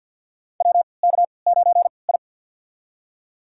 Morse code, heard as a keyed tone, about 47 wpm, spaced as for 28 wpm, 700 Hz. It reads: WK9I
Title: USA Call Signs - 4 Characters - 3x Character Spacing - 50wpm